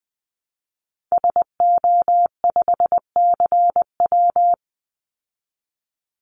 Morse code, heard as a keyed tone, about 20 wpm, 700 Hz. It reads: SO5CW
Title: Call Signs Repeated 2x - 20wpm